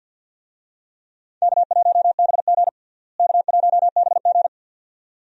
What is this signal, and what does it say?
Morse code, heard as a keyed tone, about 52 wpm, 700 Hz. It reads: K1BG K1BG
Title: USA Call Signs - 4 Characters - Repeated 2x - 50wpm